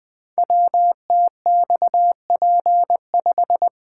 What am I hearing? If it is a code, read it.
Morse code, heard as a keyed tone, about 20 wpm, 700 Hz. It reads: WTXP5
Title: Code Groups - 5 Letter:Numbers - 20wpm